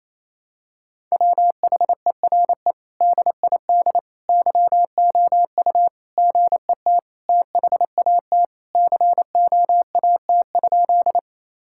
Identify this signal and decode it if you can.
Morse code, heard as a keyed tone, about 28 wpm, 700 Hz. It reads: WHERE DID YOU GET THAT COAT?
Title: Sentences from Top 1600 Words 28wpm